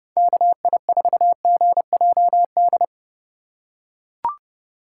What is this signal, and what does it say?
Morse code, heard as a keyed tone, about 30 wpm, 700 Hz. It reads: KI4GJD
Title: USA Call Signs - 6 Characters - 4x Character Spacing - 30wpm